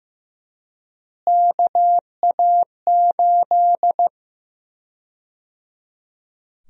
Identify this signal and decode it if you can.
Morse code, heard as a keyed tone, about 15 wpm, 700 Hz. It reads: KA8
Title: USA Call Sign Prefixes - Repeated 2x - 15wpm